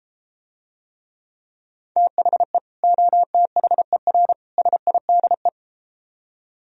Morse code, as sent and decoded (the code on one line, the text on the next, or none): - .... / --- - .... .-. / ... .. -.. .
THE OTHER SIDE